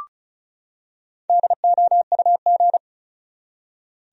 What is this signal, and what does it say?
Morse code, heard as a keyed tone, about 36 wpm, 700 Hz. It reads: DOUG
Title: USA Common Names - Max 5 Letters - 35wpm